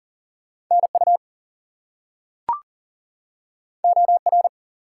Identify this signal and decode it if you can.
Morse code, heard as a keyed tone, about 40 wpm, 700 Hz. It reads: NU OR
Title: US States and Canadian Provinces 40wpm